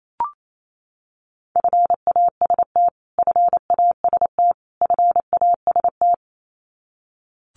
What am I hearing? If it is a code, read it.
Morse code, heard as a keyed tone, about 28 wpm, 700 Hz. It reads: FAST FAST FAST
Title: Top 500 Words Repeated 28wpm